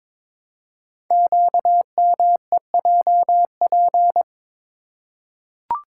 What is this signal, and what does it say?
Morse code, heard as a keyed tone, about 23 wpm, 700 Hz. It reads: QMEJP
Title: Code Groups - 5 Letter:Numbers - 22wpm